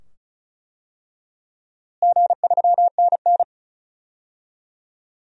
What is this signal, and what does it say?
Morse code, heard as a keyed tone, about 36 wpm, 700 Hz. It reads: G3NN
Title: DXCC-100 Call Signs - 4 Characters - Repeated 2x - 35wpm